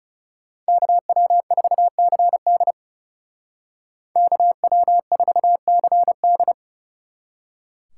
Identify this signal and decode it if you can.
Morse code, heard as a keyed tone, about 35 wpm, 700 Hz. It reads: KW4CD KW4CD
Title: Speed Racing - USA 2x2 Call Signs 30wpm